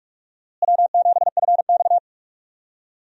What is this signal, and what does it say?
Morse code, heard as a keyed tone, about 46 wpm, 700 Hz. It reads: W7FX
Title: USA Call Signs - 4 Characters - Repeated 3x - 45wpm